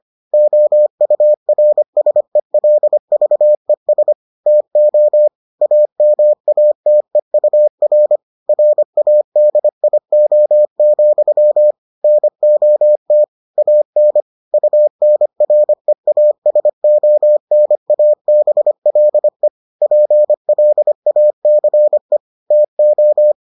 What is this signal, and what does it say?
Morse code, heard as a keyed tone, about 25 wpm, 600 Hz. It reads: OURSELVES TO AMATEUR RADIO, NOT AN UNREASONABLE PLACE TO